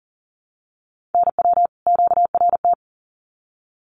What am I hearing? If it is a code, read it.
Morse code, heard as a keyed tone, about 40 wpm, 700 Hz. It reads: NW QRT
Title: Common QSO Phrases 40wpm